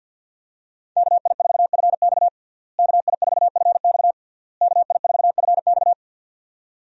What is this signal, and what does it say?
Morse code, heard as a keyed tone, about 51 wpm, 700 Hz. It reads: KI4FX KI4FX KI4FX
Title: USA Call Signs - 5 Characters - Repeated 3x - 50wpm